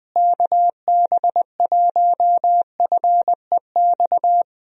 Morse code, heard as a keyed tone, about 21 wpm, 700 Hz. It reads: KB1FEX
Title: USA Call Signs - 6 Characters - Repeated 2x - 20wpm